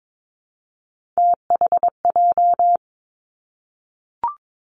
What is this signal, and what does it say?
Morse code, heard as a keyed tone, about 22 wpm, 700 Hz. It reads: THJ
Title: Three Challenging Letter-Number Combinations 22wpm